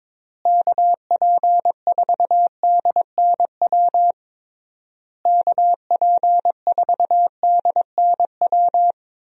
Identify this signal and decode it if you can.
Morse code, heard as a keyed tone, about 22 wpm, 700 Hz. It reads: KP4DNW KP4DNW
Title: USA Call Signs - 6 Characters - Repeated 3x - 22wpm